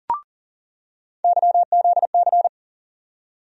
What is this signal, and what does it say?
Morse code, heard as a keyed tone, about 40 wpm, 700 Hz. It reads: YZC
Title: USA Call Sign Postfixes - 3 Letters 40wpm